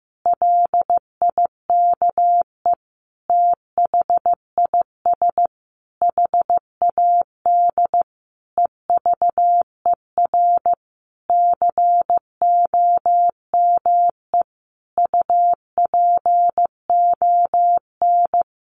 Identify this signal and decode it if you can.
Morse code, heard as a keyed tone, about 15 wpm, 700 Hz. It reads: LIKE THIS HAD EVER COME UPON
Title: Sentences from Top 1100 Words 15wpm